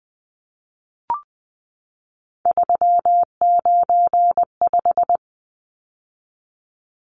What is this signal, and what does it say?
Morse code, heard as a keyed tone, about 20 wpm, 700 Hz. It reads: 395
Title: Numbers - 3-Digit 20wpm